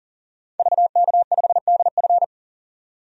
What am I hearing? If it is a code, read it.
Morse code, heard as a keyed tone, about 37 wpm, 700 Hz. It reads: VK5DF
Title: DXCC-100 Call Signs - 5 Characters - 40wpm